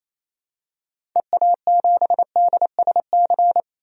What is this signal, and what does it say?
Morse code, heard as a keyed tone, about 28 wpm, 700 Hz. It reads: EA7DSC
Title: DXCC-100 Call Signs - 6 Characters - 28wpm